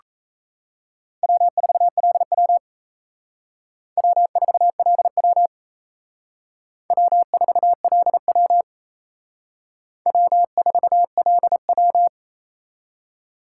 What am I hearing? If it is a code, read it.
Morse code, heard as a keyed tone, about 35 wpm, 700 Hz. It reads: W4LW W4LW W4LW W4LW